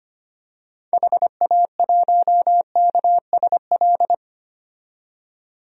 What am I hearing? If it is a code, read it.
Morse code, heard as a keyed tone, about 25 wpm, 700 Hz. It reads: HA1KSL
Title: DXCC-100 Call Signs - 6 Characters - 25wpm